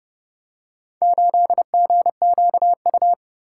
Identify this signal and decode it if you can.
Morse code, heard as a keyed tone, about 29 wpm, 700 Hz. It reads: 8GQU